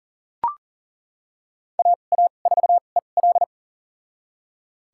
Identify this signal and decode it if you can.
Morse code, heard as a keyed tone, about 37 wpm, spaced as for 28 wpm, 700 Hz. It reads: AA4EL
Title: USA Call Signs - 5 Characters - 2x Character Spacing - 40wpm